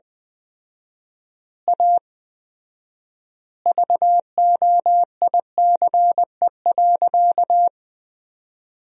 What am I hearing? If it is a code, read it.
Morse code, heard as a keyed tone, about 20 wpm, 700 Hz. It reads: A VOICE.
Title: 4x Word-Spacing - Sentences from Top 500 Words - 20wpm